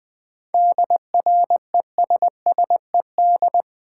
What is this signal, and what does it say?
Morse code, heard as a keyed tone, about 20 wpm, 700 Hz. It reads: DRESSED
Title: 7 Letter Words - First 3 Letters Given - 20wpm